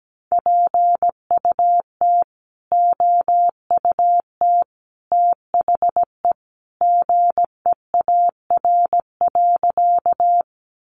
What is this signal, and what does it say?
Morse code, heard as a keyed tone, about 17 wpm, 700 Hz. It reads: PUT OUT THE GEAR.
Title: Sentences from Top 1400 Words 17wpm